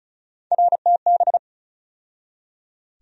Morse code, heard as a keyed tone, about 36 wpm, 700 Hz. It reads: RTB
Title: USA Call Sign Postfixes - 3 Letters 35wpm